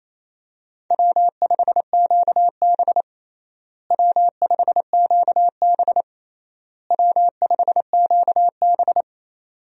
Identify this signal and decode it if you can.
Morse code, heard as a keyed tone, about 28 wpm, 700 Hz. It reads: W5QB W5QB W5QB
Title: USA Call Signs - 4 Characters - Repeated 3x - 28wpm